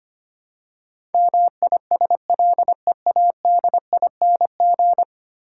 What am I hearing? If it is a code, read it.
Morse code, heard as a keyed tone, about 25 wpm, 700 Hz. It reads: MISLEADING
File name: 10 Letter Words - First 3 Letters Given - 25wpm